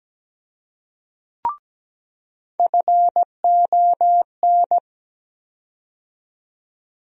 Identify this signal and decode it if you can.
Morse code, heard as a keyed tone, about 17 wpm, 700 Hz. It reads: FON
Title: USA Call Sign Postfixes - 3 Letters 17wpm